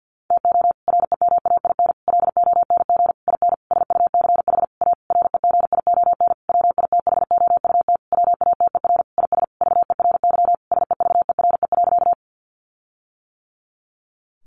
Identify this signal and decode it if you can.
Morse code, heard as a keyed tone, about 48 wpm, 700 Hz. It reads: TO REMAIN LONG IN SUCH A REGION WITHOUT WATER IS VERY SEVERE.
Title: Sentences from Top 1500 Words 50wpm